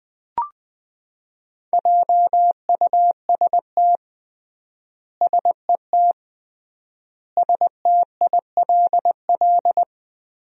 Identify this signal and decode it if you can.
Morse code, heard as a keyed tone, about 21 wpm, 700 Hz. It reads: JUST SET STILL
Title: 3x Word-Spacing - Sentences from Top 300 Words - 20wpm